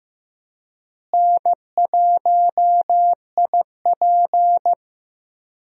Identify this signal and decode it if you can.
Morse code, heard as a keyed tone, about 15 wpm, 700 Hz. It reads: N1IP
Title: USA Call Signs - 4 Characters - Repeated 3x - 15wpm